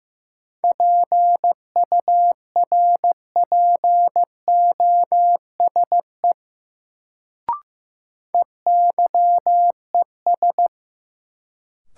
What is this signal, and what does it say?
Morse code, heard as a keyed tone, about 15 wpm, 700 Hz. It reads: PURPOSE EYES
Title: Sets of 5 Words - New Word Drill - 15wpm